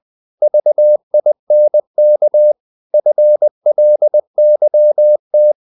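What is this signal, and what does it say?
Morse code, heard as a keyed tone, about 20 wpm, 600 Hz. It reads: VINK FLYT
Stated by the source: Morse code practice of short Danish words at 20 wpm, from CW med Gnister.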